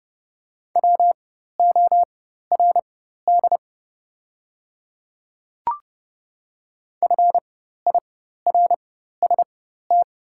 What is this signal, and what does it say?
Morse code, heard as a keyed tone, about 30 wpm, spaced as for 14 wpm, 690 Hz. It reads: WORD FIRST
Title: 4x Character Spacing - Top 500 Words - 30wpm